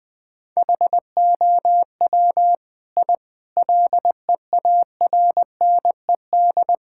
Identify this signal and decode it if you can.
Morse code, heard as a keyed tone, about 20 wpm, 700 Hz. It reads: HOW I LEARNED
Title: Sets of 8 Words - 20wpm